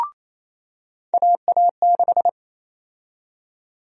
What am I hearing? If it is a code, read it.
Morse code, heard as a keyed tone, about 28 wpm, 700 Hz. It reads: AA6
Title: USA Call Sign Prefixes - 28wpm